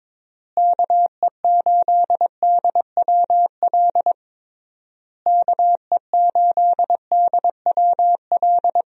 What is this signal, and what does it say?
Morse code, heard as a keyed tone, about 22 wpm, 700 Hz. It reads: KE8DWL KE8DWL
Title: USA Call Signs - 6 Characters - Repeated 3x - 22wpm